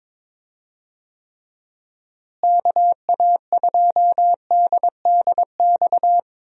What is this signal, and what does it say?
Morse code, heard as a keyed tone, about 22 wpm, 700 Hz. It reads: KA2DDX
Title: USA Call Signs - 6 Characters - Repeated 2x - 22wpm